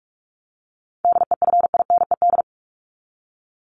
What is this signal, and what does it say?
Morse code, heard as a keyed tone, about 46 wpm, 700 Hz. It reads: DEFINED
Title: ED Suffix - Top 100 Words with Base Words 45wpm